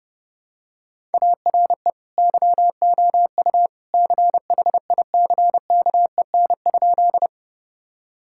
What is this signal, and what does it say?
Morse code, heard as a keyed tone, about 30 wpm, 700 Hz. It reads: ARE YOU CHICKEN?
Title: Sentences from Top 1000 Words - Review Part 5 - 30wpm